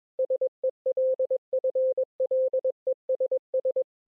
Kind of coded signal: Morse code